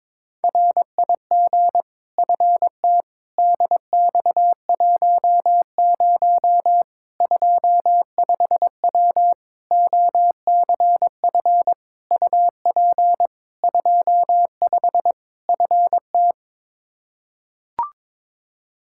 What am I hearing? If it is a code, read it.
Morse code, heard as a keyed tone, about 21 wpm, 700 Hz. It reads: RIG FT DX10 25W OCF UP 25 FT
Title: Practice QSO - Exchange 2 - Info - 22wpm